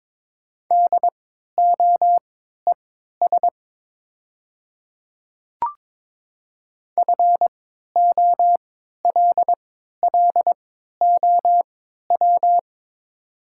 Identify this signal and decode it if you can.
Morse code, heard as a keyed tone, about 21 wpm, spaced as for 12 wpm, 700 Hz. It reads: DOES FOLLOW
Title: 3x Character Spacing - Top 200 Words - 22wpm